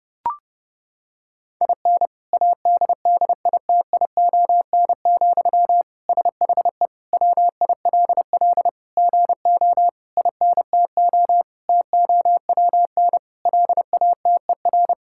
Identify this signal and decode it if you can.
Morse code, heard as a keyed tone, about 30 wpm, 700 Hz. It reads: IN ADDITION, SHE WILL GO INTO TOWN LATER
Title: Sentences from Top 1000 Words - Review Part 6 - 30wpm